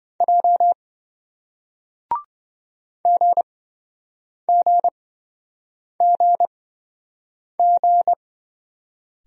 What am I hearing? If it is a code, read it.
Morse code, heard as a keyed tone, about 27 wpm, spaced as for 7 wpm, 700 Hz. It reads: J GGGG